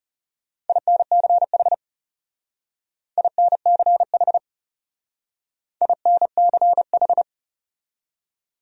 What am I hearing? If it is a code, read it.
Morse code, heard as a keyed tone, about 37 wpm, 700 Hz. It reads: INCH INCH INCH
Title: Speed-Racing Top 500 Words 30wpm